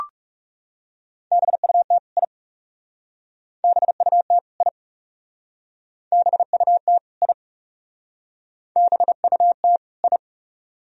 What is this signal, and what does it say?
Morse code, heard as a keyed tone, about 35 wpm, 700 Hz. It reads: BUT I BUT I BUT I BUT I
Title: Speed-Racing - Sets of 2 Words 30wpm